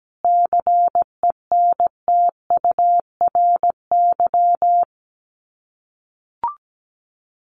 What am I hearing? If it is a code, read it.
Morse code, heard as a keyed tone, about 17 wpm, 700 Hz. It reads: CENTURY
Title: Sets of 5 Words Encore2 17wpm